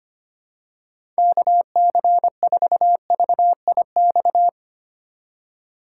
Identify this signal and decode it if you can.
Morse code, heard as a keyed tone, about 25 wpm, 700 Hz. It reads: KC4VIX
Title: USA Call Signs - 6 Characters - 4x Character Spacing - 25wpm